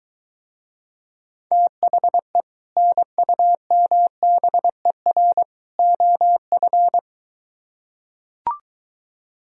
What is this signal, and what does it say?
Morse code, heard as a keyed tone, about 22 wpm, 700 Hz. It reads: THE NUMBER OF